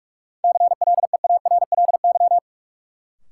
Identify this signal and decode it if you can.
Morse code, heard as a keyed tone, about 45 wpm, 700 Hz. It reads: CLEARLY
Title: Sentences from Top 1300 Words - New Word Drill - 45wpm